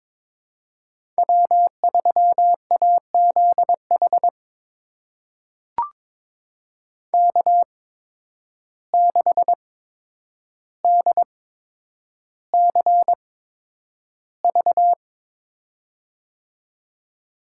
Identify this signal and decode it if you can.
Morse code, heard as a keyed tone, about 22 wpm, spaced as for 9 wpm, 700 Hz. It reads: W3AZH K6DCV
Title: USA Call Signs - 5 Characters - 8x Character Spacing - 22wpm